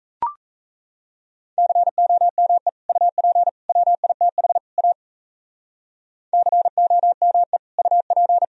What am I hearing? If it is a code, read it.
Morse code, heard as a keyed tone, about 37 wpm, 700 Hz. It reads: COME UP WITH A COME UP